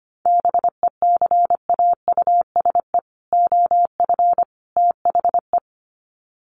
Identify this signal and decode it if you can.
Morse code, heard as a keyed tone, about 25 wpm, 700 Hz. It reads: BECAUSE OF THE
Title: Sets of 3 Words 25wpm